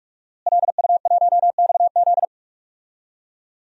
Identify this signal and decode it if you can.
Morse code, heard as a keyed tone, about 46 wpm, 700 Hz. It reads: LU1XZ
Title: DXCC-100 Call Signs - 5 Characters - Repeated 2x - 45wpm